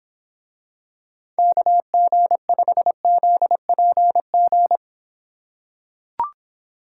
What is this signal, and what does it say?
Morse code, heard as a keyed tone, about 25 wpm, 700 Hz. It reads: KG5ZPG